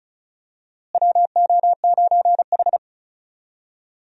Morse code, heard as a keyed tone, about 36 wpm, 700 Hz. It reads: WO9H
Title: USA Call Signs - 4 Characters - Repeated 3x - 35wpm